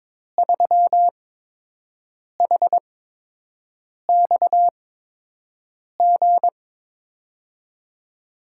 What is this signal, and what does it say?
Morse code, heard as a keyed tone, about 21 wpm, spaced as for 6 wpm, 700 Hz. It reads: 3HXG
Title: USA Call Signs - 5 Characters - 8x Character Spacing - 22wpm